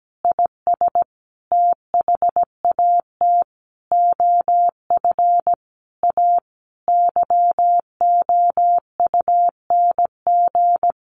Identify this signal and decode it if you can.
Morse code, heard as a keyed tone, about 17 wpm, 700 Hz. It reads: IS THAT OF A YOUNG
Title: Sentences from Top 500 Words 17wpm